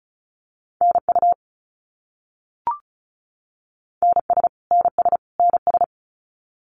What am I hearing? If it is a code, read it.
Morse code, heard as a keyed tone, about 34 wpm, 700 Hz. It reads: NU NS NS NS
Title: Canadian Provinces and Territories - Repeated 3x - 35wpm